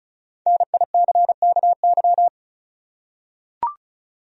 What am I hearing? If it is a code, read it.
Morse code, heard as a keyed tone, about 36 wpm, 700 Hz. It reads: NICKY
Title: USA Common Names - 35wpm